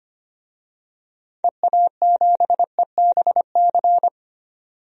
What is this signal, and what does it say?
Morse code, heard as a keyed tone, about 25 wpm, 700 Hz. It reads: EA7EBC
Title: DXCC-100 Call Signs - 6 Characters - 25wpm